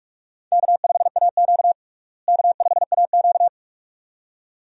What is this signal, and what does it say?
Morse code, heard as a keyed tone, about 41 wpm, 700 Hz. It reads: K5AQ K5AQ
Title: USA Call Signs - 4 Characters - Repeated 2x - 45wpm